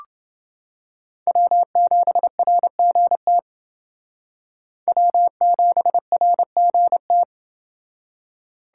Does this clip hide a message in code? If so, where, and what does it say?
Yes: Morse code from 1.3 to 7.2 s, heard as a keyed tone, about 28 wpm, 700 Hz. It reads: W7RGT W7RGT